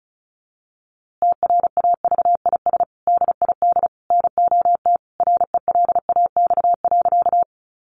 Morse code, heard as a keyed tone, about 34 wpm, 700 Hz. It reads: TRAVIS DID NOT RELAX.
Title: A Sound Of Thunder 35wpm